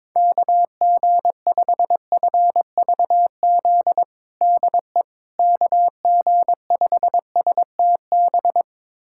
Morse code, heard as a keyed tone, about 22 wpm, 700 Hz. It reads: KG5FVZ DE KG5STB
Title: USA Callsign DE Callsign - 6 Characters - 22wpm